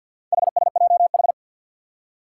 Morse code, heard as a keyed tone, about 52 wpm, 700 Hz. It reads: HSJH
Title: Four Challenging Letter-Number Combinations 50wpm